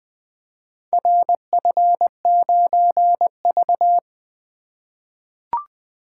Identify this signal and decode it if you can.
Morse code, heard as a keyed tone, about 20 wpm, 700 Hz. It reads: RF9V
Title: Four Letter-Number Combinations - 20wpm